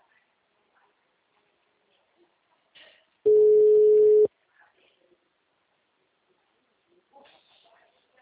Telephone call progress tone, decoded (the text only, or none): ringback tone